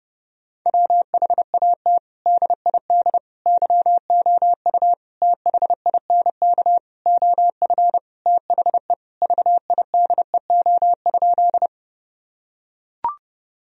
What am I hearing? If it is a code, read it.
Morse code, heard as a keyed tone, about 30 wpm, 700 Hz. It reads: WHAT DID YOU THINK OF THE VIDEO?
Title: Sentences from Top 1000 Words - Review Part 4 - 30wpm